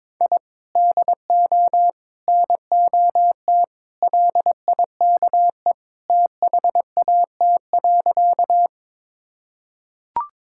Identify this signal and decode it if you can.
Morse code, heard as a keyed tone, about 22 wpm, 700 Hz. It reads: I DO NOT LIKE THAT.
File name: practice / 22wpm / Sentences with 4-Word Phrases - 22wpm